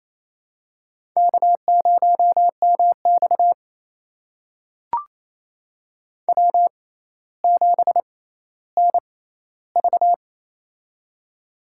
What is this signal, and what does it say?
Morse code, heard as a keyed tone, about 28 wpm, spaced as for 14 wpm, 700 Hz. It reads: K0MX W7NV
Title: USA Call Signs - 6x Character Spacing - 28wpm